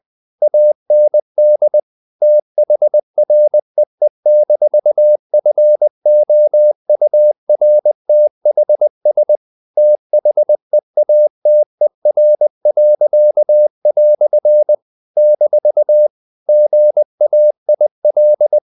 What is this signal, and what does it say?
Morse code, heard as a keyed tone, about 20 wpm, 600 Hz. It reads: AND THREE-FOURTHS THEATER." - GAIL
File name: CW / Practice / EnglishQuotations0320